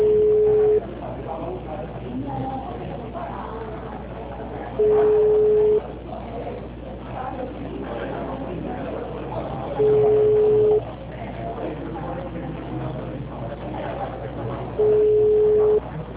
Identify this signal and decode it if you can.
Ringback tone (Europe, 425 Hz), with speech.